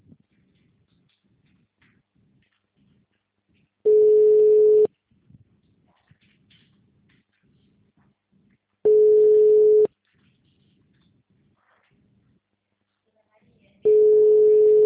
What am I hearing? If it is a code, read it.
Ringback tone (Europe, 425 Hz), with speech.